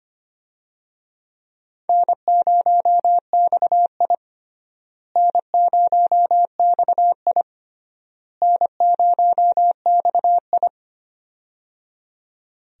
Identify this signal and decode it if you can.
Morse code, heard as a keyed tone, about 25 wpm, 700 Hz. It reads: N0XI N0XI N0XI
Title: USA Call Signs - 4 Characters - Repeated 3x - 25wpm